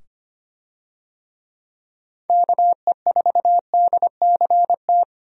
Morse code, heard as a keyed tone, about 25 wpm, 700 Hz. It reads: KE4DCT
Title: USA Call Signs - 6 Characters - 25wpm